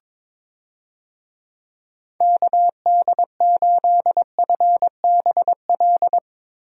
Morse code, heard as a keyed tone, about 22 wpm, 700 Hz. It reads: KD8FBL
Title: USA Call Signs - 6 Characters - 22wpm